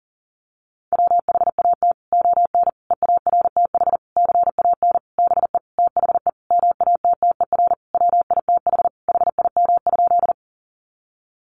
Morse code, heard as a keyed tone, about 40 wpm, 700 Hz. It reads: WHAT ON EARTH CAN BE THE MATTER WITH HIM?
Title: Sentences from Top 700 Words 40wpm